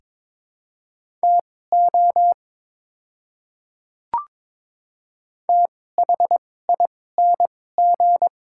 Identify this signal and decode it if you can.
Morse code, heard as a keyed tone, about 22 wpm, spaced as for 16 wpm, 700 Hz. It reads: TO THING
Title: 2x Character Spacing - Top 100 Words - 22wpm